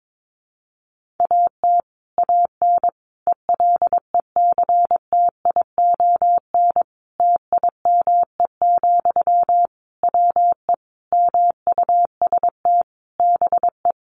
Morse code, heard as a keyed tone, about 22 wpm, 700 Hz. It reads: AT AN ELECTION TIME, WE MUST BE
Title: Sentences from Top 1500 Words 22wpm